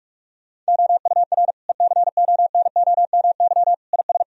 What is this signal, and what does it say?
Morse code, heard as a keyed tone, about 46 wpm, 700 Hz. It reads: OUR ECONOMY IS
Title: Sentences from Top 1000 Words - Review Part 3 - 45wpm